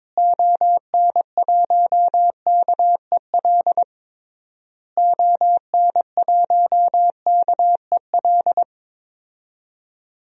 Morse code, heard as a keyed tone, about 22 wpm, 700 Hz. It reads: ON1KEL ON1KEL
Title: DXCC-100 Call Signs - 6 Characters - Repeated 3x - 22wpm